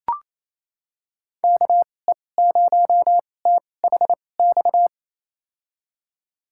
Morse code, heard as a keyed tone, about 28 wpm, spaced as for 20 wpm, 700 Hz. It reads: KE0THX
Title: USA Call Signs - 6 Characters - 2x Character Spacing - 28wpm